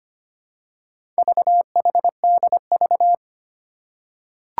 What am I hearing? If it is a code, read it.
Morse code, heard as a keyed tone, about 25 wpm, 700 Hz. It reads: VHDV
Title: Four Challenging Letter-Number Combinations 25wpm